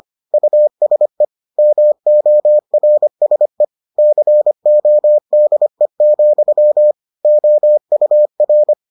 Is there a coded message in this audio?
Yes: Morse code from 0.3 to 8.7 s, heard as a keyed tone, about 25 wpm, 600 Hz. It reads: USE MORSE CODE, OUR